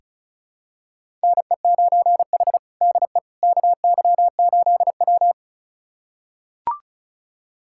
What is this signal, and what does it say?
Morse code, heard as a keyed tone, about 36 wpm, 700 Hz. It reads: NE9H DE KY8W
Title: USA Callsign DE Callsign - 4 Characters - 35wpm